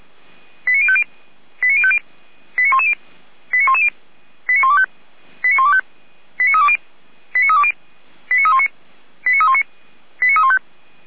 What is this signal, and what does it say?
Five-tone selective call (ZVEI), heard as quick runs of five tones, tones of about 70 ms. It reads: 89050 89050 89110 89110 89125 89125 89320 89320 89319 89319 89315